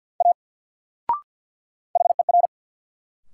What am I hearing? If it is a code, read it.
Morse code, heard as a keyed tone, about 50 wpm, 700 Hz. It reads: A HER
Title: Sentences from Top 100 Words - New Word Drill - 50wpm